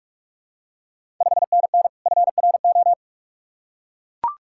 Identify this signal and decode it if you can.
Morse code, heard as a keyed tone, about 44 wpm, 690 Hz. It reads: 5NN FRO
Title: Ontario QSO Party Exchange 45wpm